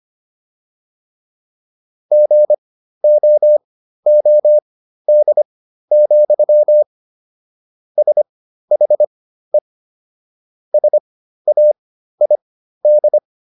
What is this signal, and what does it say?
Morse code, heard as a keyed tone, about 25 wpm, spaced as for 13 wpm, 600 Hz. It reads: GOOD, SHE SAID